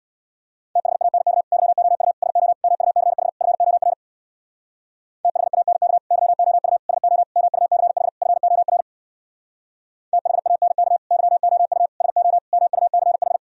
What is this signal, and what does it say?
Morse code, heard as a keyed tone, about 65 wpm, 700 Hz. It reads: THANK YOU SO MUCH FOR THANK YOU SO MUCH FOR THANK YOU SO MUCH